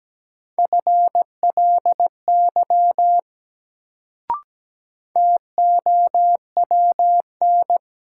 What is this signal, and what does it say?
Morse code, heard as a keyed tone, about 17 wpm, 700 Hz. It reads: FLY TOWN
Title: Sentences from Top 500 Words - New Word Drill - 17wpm